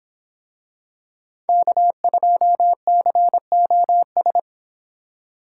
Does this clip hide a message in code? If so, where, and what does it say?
Yes: Morse code from 1.5 to 4.4 s, heard as a keyed tone, about 26 wpm, 700 Hz. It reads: K2COS